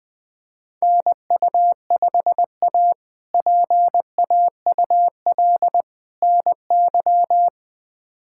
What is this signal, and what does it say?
Morse code, heard as a keyed tone, about 20 wpm, 700 Hz. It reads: NU5A PAUL NY
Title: K1USN SST QSO Practice - 20wpm